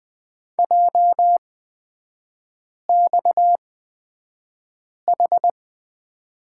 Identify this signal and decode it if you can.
Morse code, heard as a keyed tone, about 20 wpm, spaced as for 5 wpm, 700 Hz. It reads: JXH
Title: 229 Lesson 37e - Letter x - Three Characters 20wpm